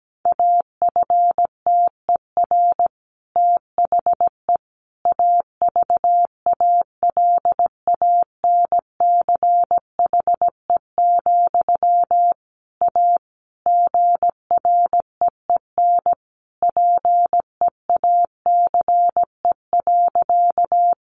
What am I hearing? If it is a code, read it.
Morse code, heard as a keyed tone, about 17 wpm, 700 Hz. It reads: AFTER THE AVALANCHE, A GREEN PEACE.
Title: A Sound Of Thunder 17wpm